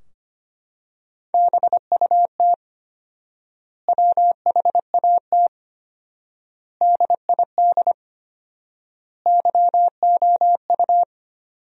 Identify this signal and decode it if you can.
Morse code, heard as a keyed tone, about 25 wpm, 700 Hz. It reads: BUT WHAT DID YOU
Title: 4x Word-Spacing - Sentences from Top 100 Words - 25wpm